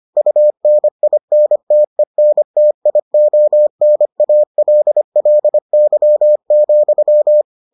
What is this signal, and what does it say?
Morse code, heard as a keyed tone, about 25 wpm, 600 Hz. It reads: UNINTENTIONALLY,